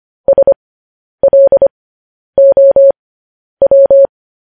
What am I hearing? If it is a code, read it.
Morse code, heard as a keyed tone, about 25 wpm, spaced as for 10 wpm, 550 Hz. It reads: SLOW
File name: podcast / News headlines-1x-25x10